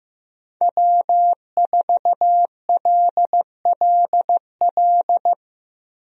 Morse code, heard as a keyed tone, about 15 wpm, 700 Hz. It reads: W4LLL
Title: USA Call Signs - 5 Characters - 15wpm